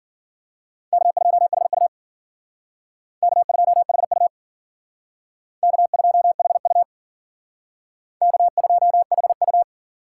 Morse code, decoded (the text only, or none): K2HU K2HU K2HU K2HU